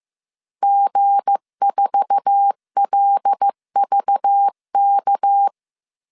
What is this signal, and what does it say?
Morse code, heard as a keyed tone, about 15 wpm, 800 Hz. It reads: G4LVK
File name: CW / DET / 1 9 1500C S A 14wpm 3222